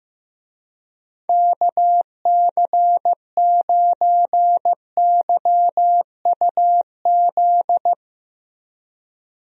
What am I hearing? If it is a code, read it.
Morse code, heard as a keyed tone, about 15 wpm, 700 Hz. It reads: KC9YUZ
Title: USA Call Signs - 6 Characters - Repeated 2x - 15wpm